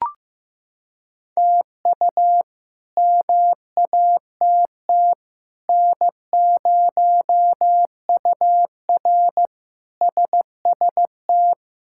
Morse code, heard as a keyed tone, about 15 wpm, 700 Hz. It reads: TU MATT N0UR SST
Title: K1USN SST QSO Practice - 15wpm